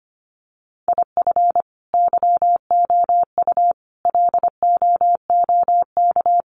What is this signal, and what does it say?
Morse code, heard as a keyed tone, about 25 wpm, 700 Hz. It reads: IF YOU LOOK
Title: Sets of 5 Words 25wpm